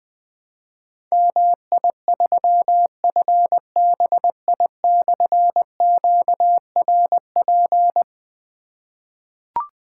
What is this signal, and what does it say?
Morse code, heard as a keyed tone, about 20 wpm, 700 Hz. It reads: MI3FBI/QRP
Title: Difficult Call Signs 20wpm